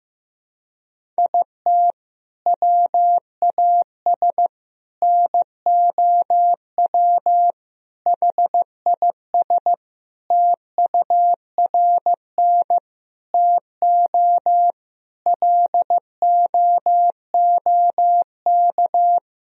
Morse code, heard as a keyed tone, about 15 wpm, 700 Hz. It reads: IT WAS NOW HIS TURN TO LOOK
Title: Sentences from Top 200 Words and Abbreviations 15wpm